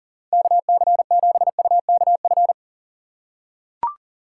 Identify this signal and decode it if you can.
Morse code, heard as a keyed tone, about 40 wpm, 700 Hz. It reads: KC7UKF
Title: USA Call Signs - 6 Characters - 6x Character Spacing - 40wpm